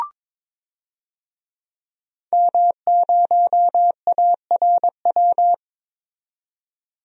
Morse code, heard as a keyed tone, about 22 wpm, 700 Hz. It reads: M0ARW
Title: DXCC-100 Call Signs - 5 Characters - 22wpm